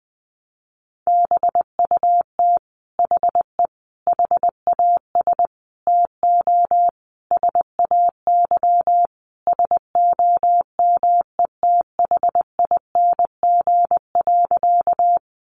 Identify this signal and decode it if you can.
Morse code, heard as a keyed tone, about 20 wpm, 700 Hz. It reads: BUT HE HAS TO SAY SOMETHING.
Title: Sentences from Top 600 Words 20wpm